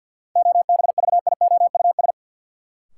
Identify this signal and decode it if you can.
Morse code, heard as a keyed tone, about 51 wpm, 700 Hz. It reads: OBVIOUS
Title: Sentences from Top 1600 Words - New Word Drill - 50wpm